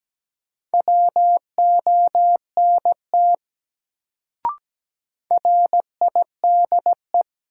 Morse code, heard as a keyed tone, about 17 wpm, 700 Hz. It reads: WONT RIDE
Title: Sentences from Top 1000 Words - Review Part 6 - New Word Drill - 17wpm